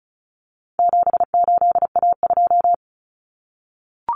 Morse code, heard as a keyed tone, about 34 wpm, 700 Hz. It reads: 78A2
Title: Common Cut Numbers - 4-Digit 35wpm